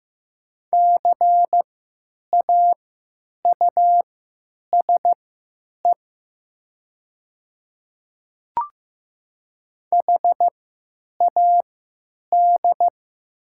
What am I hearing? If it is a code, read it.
Morse code, heard as a keyed tone, about 15 wpm, spaced as for 9 wpm, 700 Hz. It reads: CAUSE HAD
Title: 3x Character Spacing - Top 200 Words - 15wpm